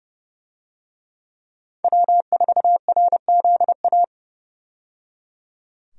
Morse code, heard as a keyed tone, about 30 wpm, 700 Hz. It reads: W4RZA
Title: USA Call Signs - 5 Characters - 30wpm